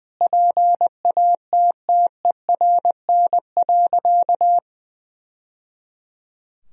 Morse code, heard as a keyed tone, about 20 wpm, 700 Hz. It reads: PATTERN.
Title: Sentences from Top 1000 Words - Review Part 2 - 20wpm